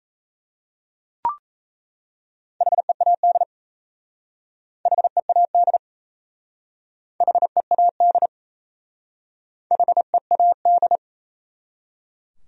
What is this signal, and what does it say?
Morse code, heard as a keyed tone, about 37 wpm, 700 Hz. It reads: HEAD HEAD HEAD HEAD